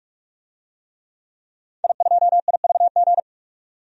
Morse code, heard as a keyed tone, about 46 wpm, 700 Hz. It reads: I2IVG
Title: DXCC-100 Call Signs - 5 Characters - 45wpm